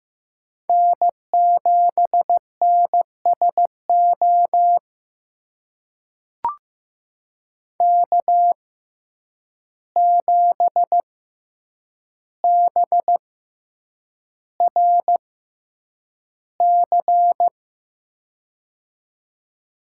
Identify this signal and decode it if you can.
Morse code, heard as a keyed tone, about 15 wpm, spaced as for 8 wpm, 700 Hz. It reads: N7NSO K7BRC
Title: USA Call Signs - 5 Characters - 6x Character Spacing - 15wpm